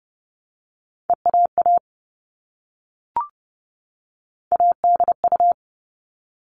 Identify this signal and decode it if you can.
Morse code, heard as a keyed tone, about 30 wpm, 700 Hz. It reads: EAA ADU
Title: All Cut Numbers - 3-Digit 30wpm